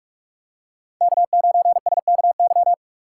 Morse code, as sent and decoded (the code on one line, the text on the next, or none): -.- ----. ... -.- -.--
K9SKY